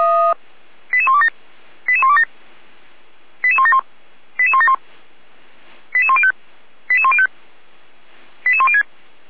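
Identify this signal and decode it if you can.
Five-tone selective call (ZVEI), heard as quick runs of five tones, tones of about 70 ms. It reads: 88127 88127 88171 88171 88105 88105 88106
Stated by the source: Five-tone ZVEI selective call 88119 received on an SDR.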